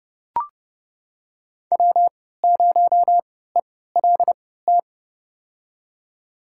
Morse code, heard as a keyed tone, about 32 wpm, spaced as for 17 wpm, 700 Hz. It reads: W0ELT
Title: USA Call Signs - 5 Characters - 3x Character Spacing - 30wpm